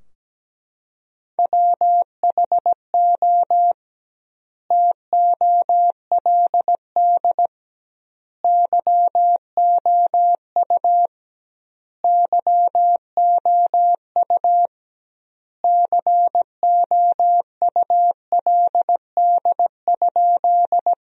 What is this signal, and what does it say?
Morse code, heard as a keyed tone, about 17 wpm, 700 Hz. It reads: WHO TOLD YOU YOU COULD?
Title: 2x Word-Spacing - Sentences from Top 400 Words - 17wpm